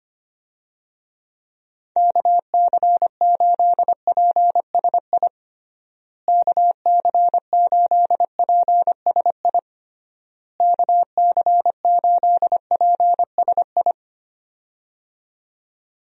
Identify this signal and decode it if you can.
Morse code, heard as a keyed tone, about 25 wpm, 700 Hz. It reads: KC8PSI KC8PSI KC8PSI
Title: USA Call Signs - 6 Characters - Repeated 3x - 25wpm